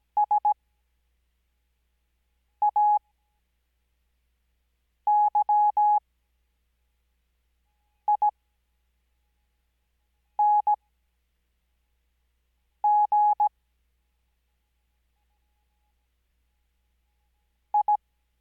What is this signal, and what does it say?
Morse code, heard as a keyed tone, about 17 wpm, spaced as for 4 wpm, 840 Hz. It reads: SAYING I
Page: Farnsworth Spacing: 15WPM Character speed Spaced at 5WPM.